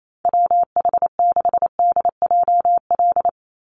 Morse code, heard as a keyed tone, about 28 wpm, 700 Hz. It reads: WH6DJL
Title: USA Call Signs 28wpm